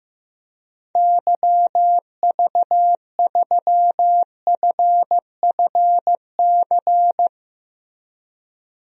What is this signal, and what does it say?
Morse code, heard as a keyed tone, about 15 wpm, 700 Hz. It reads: YV3FFC
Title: DXCC-100 Call Signs - 6 Characters - Repeated 3x - 15wpm